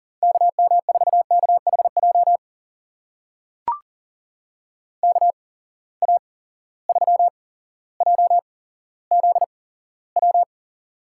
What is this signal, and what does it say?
Morse code, heard as a keyed tone, about 37 wpm, spaced as for 17 wpm, 700 Hz. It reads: KM4KHJ KA3JZW
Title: USA Call Signs - 6 Characters - 8x Character Spacing - 40wpm